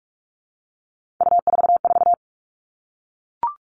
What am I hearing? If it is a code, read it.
Morse code, heard as a keyed tone, about 44 wpm, 700 Hz. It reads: U44
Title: All Cut Numbers - 3-Digit 45wpm